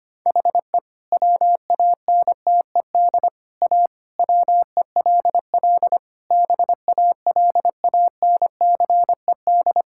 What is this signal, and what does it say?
Morse code, heard as a keyed tone, about 25 wpm, 700 Hz. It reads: HE WANTED A WELL BALANCED
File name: practice / 25wpm / All-Age Appropriate Jokes - 25wpm